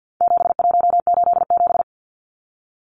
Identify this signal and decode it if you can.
Morse code, heard as a keyed tone, about 50 wpm, 700 Hz. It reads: 7187
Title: Numbers - 4-Digit 50wpm